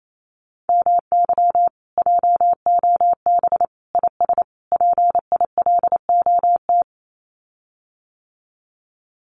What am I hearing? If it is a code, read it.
Morse code, heard as a keyed tone, about 28 wpm, 700 Hz. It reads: MY JOB IS PILOT
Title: Common QSO Phrases Encore 28wpm